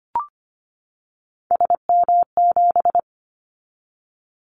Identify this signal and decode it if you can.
Morse code, heard as a keyed tone, about 25 wpm, 700 Hz. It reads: SM7
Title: Callsign Prefixes for Top 25 DXCC - 25wpm